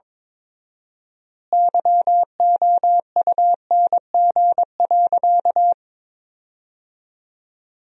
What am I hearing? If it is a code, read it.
Morse code, heard as a keyed tone, about 22 wpm, 700 Hz. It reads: YOUNG.
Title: 4x Word-Spacing - Sentences from Top 400 Words - 22wpm